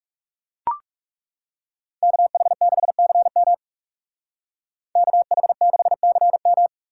Morse code, heard as a keyed tone, about 37 wpm, 700 Hz. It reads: KH6CM KH6CM
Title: Speed Racing - USA 2x2 Call Signs 30wpm